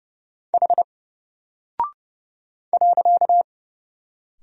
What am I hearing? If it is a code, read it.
Morse code, heard as a keyed tone, about 30 wpm, 700 Hz. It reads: H .
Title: Single Character (Letter:Number:Punctuation:Prosign) 30wpm